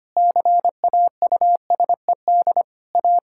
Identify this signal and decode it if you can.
Morse code, heard as a keyed tone, about 25 wpm, 700 Hz. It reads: CAUSED A
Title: Pangram Challenge v2 - 25wpm